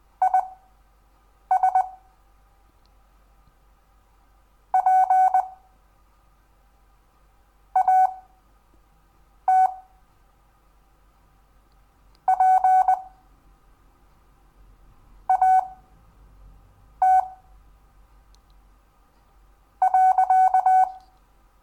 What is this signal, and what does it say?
Morse code, heard as a keyed tone, about 20 wpm, spaced as for 4 wpm, 750 Hz. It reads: IS PAT PAT.
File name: cw / Lesson10 REVIEW